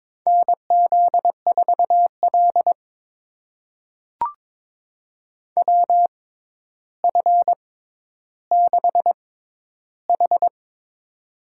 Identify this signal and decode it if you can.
Morse code, heard as a keyed tone, about 22 wpm, spaced as for 11 wpm, 700 Hz. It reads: NZ4L WF6H
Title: USA Call Signs - 6x Character Spacing - 22wpm